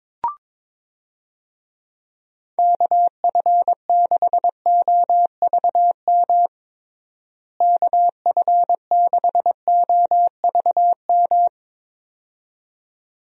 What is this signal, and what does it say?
Morse code, heard as a keyed tone, about 22 wpm, 700 Hz. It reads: KF6OVM KF6OVM
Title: USA Call Signs - 6 Characters - Repeated 2x - 22wpm